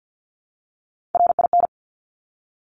Morse code, heard as a keyed tone, about 50 wpm, 700 Hz. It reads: RIN
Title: Top Three Letter Combinations in Words 50wpm